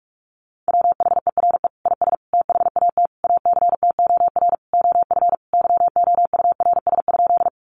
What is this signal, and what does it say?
Morse code, heard as a keyed tone, about 44 wpm, 700 Hz. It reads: WHERE IS THAT ACTOR OF YOURS?
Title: Sentences from Top 1500 Words 45wpm